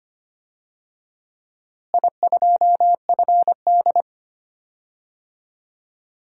Morse code, heard as a keyed tone, about 25 wpm, 700 Hz. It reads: I2FD
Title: DXCC-100 Call Signs - 4 Characters - Repeated 3x - 25wpm